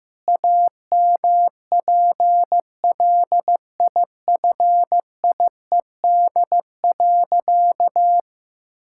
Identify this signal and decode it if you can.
Morse code, heard as a keyed tone, about 16 wpm, 700 Hz. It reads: AMPLIFIED.
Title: Pangram Challenge v2 - 15wpm